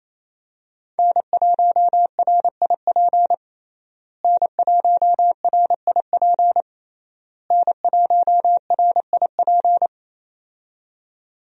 Morse code, heard as a keyed tone, about 28 wpm, 700 Hz. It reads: N1RIP N1RIP N1RIP
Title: USA Call Signs - 5 Characters - Repeated 3x - 28wpm